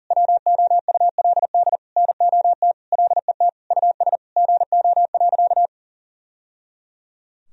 Morse code, heard as a keyed tone, about 40 wpm, 700 Hz. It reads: WOULD NOT LET US GO.
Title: Sentences from Top 1000 Words - Review Part 2 - 40wpm